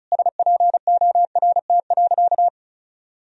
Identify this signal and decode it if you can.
Morse code, heard as a keyed tone, about 36 wpm, 700 Hz. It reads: SPORT.
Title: Sentences from Top 1000 Words - Review Part 4 - 35wpm